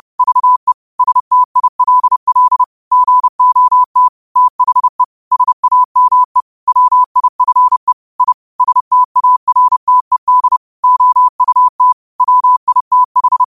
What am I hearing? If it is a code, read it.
Morse code, heard as a keyed tone, about 30 wpm, 1000 Hz. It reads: VE STILL GOT THE SAME WIFE I STARTED OUT WITH